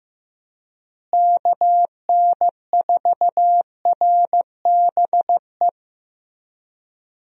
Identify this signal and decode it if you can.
Morse code, heard as a keyed tone, about 15 wpm, 700 Hz. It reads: KN4RBE
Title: USA Call Signs - 6 Characters - Repeated 3x - 15wpm